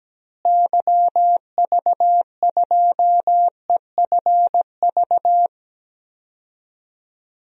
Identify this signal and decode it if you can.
Morse code, heard as a keyed tone, about 17 wpm, 700 Hz. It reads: YV2EFV
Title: DXCC-100 Call Signs - 6 Characters - Repeated 2x - 17wpm